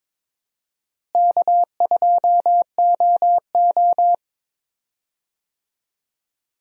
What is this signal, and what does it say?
Morse code, heard as a keyed tone, about 22 wpm, 700 Hz. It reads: K2OO
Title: USA Call Signs - 4 Characters - Repeated 2x - 22wpm